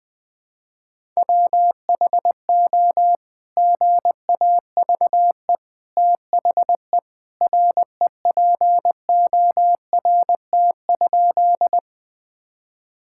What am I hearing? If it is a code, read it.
Morse code, heard as a keyed tone, about 21 wpm, 700 Hz. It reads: WHO GAVE THE REPORT?
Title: Sentences from Top 1000 Words - Review Part 6 - 20wpm